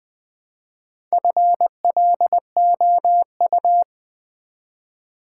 Morse code, heard as a keyed tone, about 20 wpm, 700 Hz. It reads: FLOU
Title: Build A Word - 1 Letter At A Time - 5-Letter Words - 20wpm